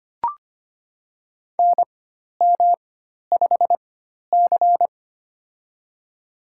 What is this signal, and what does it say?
Morse code, heard as a keyed tone, about 25 wpm, spaced as for 12 wpm, 690 Hz. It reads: NM5C
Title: USA Call Signs - 4 Characters - 4x Character Spacing - 25wpm